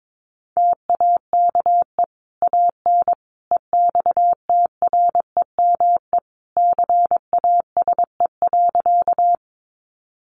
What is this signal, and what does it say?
Morse code, heard as a keyed tone, about 22 wpm, 700 Hz. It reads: TAKE AN EXTREME CASE.
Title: Sentences from Top 1300 Words 22wpm